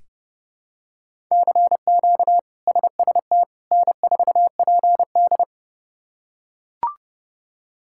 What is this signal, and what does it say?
Morse code, heard as a keyed tone, about 30 wpm, 700 Hz. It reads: CQ SST N4PD
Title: K1USN SST QSO Practice - 30wpm